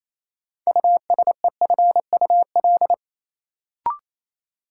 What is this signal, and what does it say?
Morse code, heard as a keyed tone, about 28 wpm, 700 Hz. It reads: USEFUL
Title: Sentences from Top 700 Words - New Word Drill - 28wpm